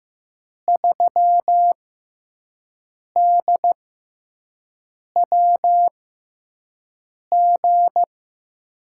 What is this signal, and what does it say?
Morse code, heard as a keyed tone, about 15 wpm, spaced as for 5 wpm, 700 Hz. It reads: 3DWG
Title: USA Call Signs - 6 Characters - 6x Character Spacing - 15wpm